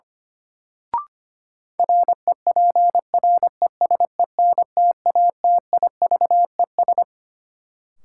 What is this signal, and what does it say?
Morse code, heard as a keyed tone, about 25 wpm, 700 Hz. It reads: REPRESENTATIVES
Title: Sets of 5 Words Encore - New Word Drill - 25wpm